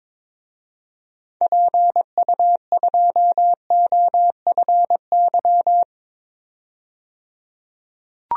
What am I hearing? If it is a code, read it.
Morse code, heard as a keyed tone, about 21 wpm, 700 Hz. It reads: PU2OFY
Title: DXCC-100 Call Signs - 6 Characters - 22wpm